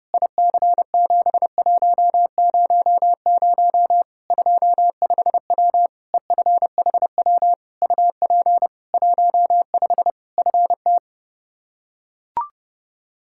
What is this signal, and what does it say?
Morse code, heard as a keyed tone, about 30 wpm, 700 Hz. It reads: IC7100 25W EFHW UP 15 FT
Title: Practice QSO - Exchange 2 - Info - 30wpm